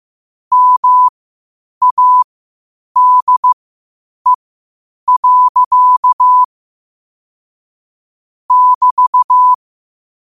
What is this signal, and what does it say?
Morse code, heard as a keyed tone, about 15 wpm, spaced as for 8 wpm, 1000 Hz. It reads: MADE. <BT>
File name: qotdfiles / 8wpm